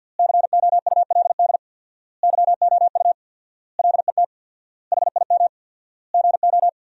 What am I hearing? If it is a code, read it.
Morse code, heard as a keyed tone, about 50 wpm, 700 Hz. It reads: COULD YOU LET HIM GO
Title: 4x Word-Spacing - Sentences from Top 300 Words - 50wpm